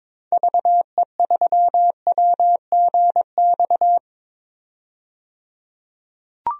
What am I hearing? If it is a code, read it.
Morse code, heard as a keyed tone, about 22 wpm, 700 Hz. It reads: VE3WGX
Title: DXCC-100 Call Signs - 6 Characters - 22wpm